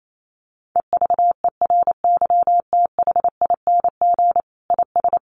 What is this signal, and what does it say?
Morse code, heard as a keyed tone, about 28 wpm, 700 Hz. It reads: EVERYTHING IS
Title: Sentences from Top 1400 Words 28wpm